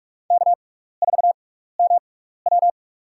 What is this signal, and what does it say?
Morse code, heard as a keyed tone, about 46 wpm, spaced as for 15 wpm, 700 Hz. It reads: K4MW
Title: USA Call Signs - 6x Character Spacing - 45wpm